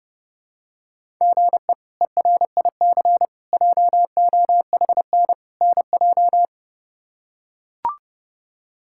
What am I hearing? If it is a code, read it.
Morse code, heard as a keyed tone, about 32 wpm, 700 Hz. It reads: GE ERIC JOHN NJ
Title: K1USN SST QSO Practice - 30wpm